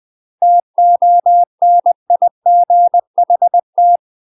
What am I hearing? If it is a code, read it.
Morse code, heard as a keyed tone, about 20 wpm, 700 Hz. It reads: TONIGHT
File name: UK MASTER QSOs TRAINING / 20WPM / UK MASTER QSOs 0050